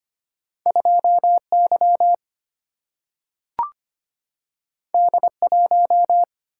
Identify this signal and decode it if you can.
Morse code, heard as a keyed tone, about 25 wpm, 700 Hz. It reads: 2 Y D 1
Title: Single-Letter Single-Number 25wpm